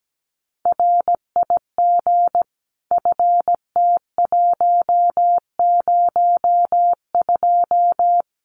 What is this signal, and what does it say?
Morse code, heard as a keyed tone, about 17 wpm, 700 Hz. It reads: RIG FT102
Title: Common QSO Phrases 17wpm